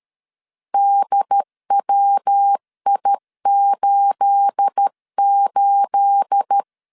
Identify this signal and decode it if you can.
Morse code, heard as a keyed tone, about 13 wpm, 800 Hz. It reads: DWI88
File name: LEZIONI / 9 28 12wpm 2351